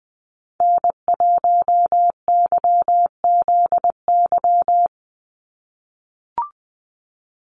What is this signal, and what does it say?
Morse code, heard as a keyed tone, about 20 wpm, 700 Hz. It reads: N1YZY
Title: USA Call Signs 20wpm